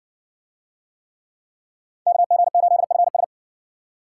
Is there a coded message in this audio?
Yes: Morse code from 2.1 to 3.2 s, heard as a keyed tone, about 60 wpm, 700 Hz. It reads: KG8FS